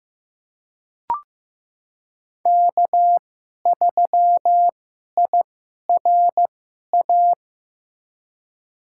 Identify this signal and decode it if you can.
Morse code, heard as a keyed tone, about 15 wpm, spaced as for 11 wpm, 700 Hz. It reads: K3IRA
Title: USA Call Signs - 5 Characters - 2x Character Spacing - 15wpm